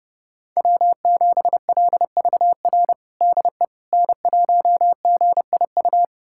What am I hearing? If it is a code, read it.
Morse code, heard as a keyed tone, about 30 wpm, 700 Hz. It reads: W7LVR DE N1GIU
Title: USA Callsign DE Callsign - 5 Characters - 30wpm